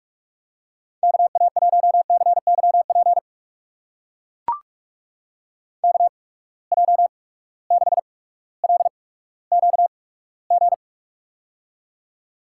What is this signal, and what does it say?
Morse code, heard as a keyed tone, about 44 wpm, spaced as for 19 wpm, 700 Hz. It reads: KA1CYP KJ6LQG
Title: USA Call Signs - 6 Characters - 8x Character Spacing - 45wpm